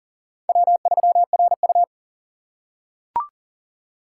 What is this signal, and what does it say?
Morse code, heard as a keyed tone, about 40 wpm, 700 Hz. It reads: W3RU
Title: USA Call Signs - 4x Character Spacing - 40wpm